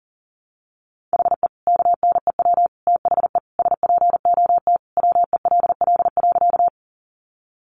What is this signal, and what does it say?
Morse code, heard as a keyed tone, about 40 wpm, 700 Hz. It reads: HE KNEW THE SPOT WELL.
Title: Sentences from Top 1100 Words 40wpm